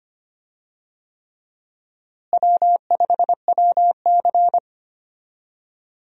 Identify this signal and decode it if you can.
Morse code, heard as a keyed tone, about 25 wpm, 700 Hz. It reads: W5WC
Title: USA Call Signs - 4 Characters - Repeated 3x - 25wpm